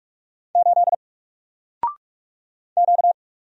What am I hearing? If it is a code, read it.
Morse code, heard as a keyed tone, about 44 wpm, 700 Hz. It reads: 8 Q
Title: Single Character (Letter:Number:Punctuation:Prosign) 45wpm